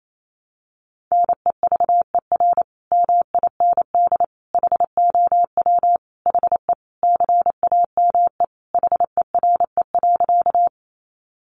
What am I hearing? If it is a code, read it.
Morse code, heard as a keyed tone, about 28 wpm, 700 Hz. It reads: NEVER MIND HOW HE CAME HERE.
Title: Sentences from Top 500 Words 28wpm